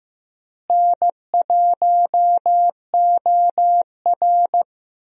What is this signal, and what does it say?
Morse code, heard as a keyed tone, about 16 wpm, 700 Hz. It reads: N1OR
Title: USA Call Signs - 4 Characters - Repeated 2x - 15wpm